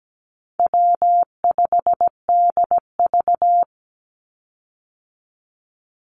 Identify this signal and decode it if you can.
Morse code, heard as a keyed tone, about 17 wpm, 700 Hz. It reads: W5DV
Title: Call Signs Repeated 3x - 17wpm